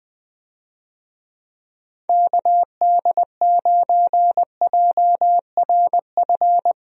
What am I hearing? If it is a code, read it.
Morse code, heard as a keyed tone, about 20 wpm, 700 Hz. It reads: KD9JRF
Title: USA Call Signs - 6 Characters - Repeated 3x - 20wpm